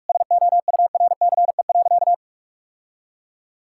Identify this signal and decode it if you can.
Morse code, heard as a keyed tone, about 42 wpm, 700 Hz. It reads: SOURCE.
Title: Sentences from Top 1000 Words - Review Part 5 - 45wpm